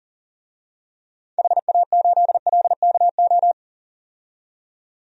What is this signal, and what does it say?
Morse code, heard as a keyed tone, about 40 wpm, 700 Hz. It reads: HA8LKO
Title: DXCC-100 Call Signs - 6 Characters - 40wpm